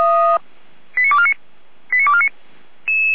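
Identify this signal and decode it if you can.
Five-tone selective call (ZVEI), heard as quick runs of five tones, tones of about 70 ms. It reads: 89249 89249
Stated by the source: ZVEI five-tone call 89229 received by SDR.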